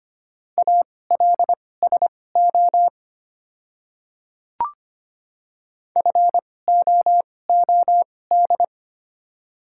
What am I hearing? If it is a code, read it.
Morse code, heard as a keyed tone, about 25 wpm, spaced as for 18 wpm, 690 Hz. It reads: ALSO FOOD
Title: 2x Character Spacing - Top 500 Words - 25wpm